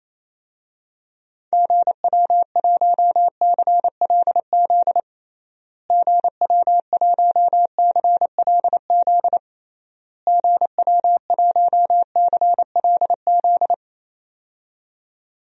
Morse code, heard as a keyed tone, about 28 wpm, 700 Hz. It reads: GW1CLZ GW1CLZ GW1CLZ
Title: DXCC-100 Call Signs - 6 Characters - Repeated 3x - 28wpm